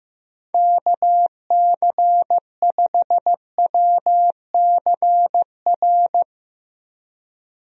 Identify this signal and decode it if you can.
Morse code, heard as a keyed tone, about 15 wpm, 700 Hz. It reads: KC5WCR
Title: USA Call Signs - 6 Characters - Repeated 2x - 15wpm